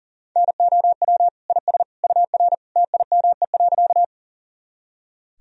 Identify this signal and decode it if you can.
Morse code, heard as a keyed tone, about 40 wpm, 700 Hz. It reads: NOW IS UR TIME.
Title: Sentences from Top 100 Words and Abbreviations 40wpm